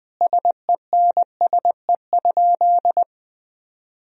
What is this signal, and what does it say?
Morse code, heard as a keyed tone, about 20 wpm, 700 Hz. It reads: SENSE?
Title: Sentences from Top 1000 Words - Review Part 4 - 20wpm